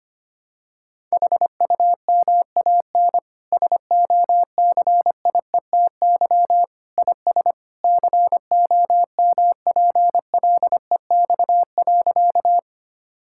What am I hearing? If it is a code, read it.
Morse code, heard as a keyed tone, about 25 wpm, 700 Hz. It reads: HUMAN SOCIETY IS COMPLEX.
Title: Sentences from Top 1000 Words - Review Part 6 - 25wpm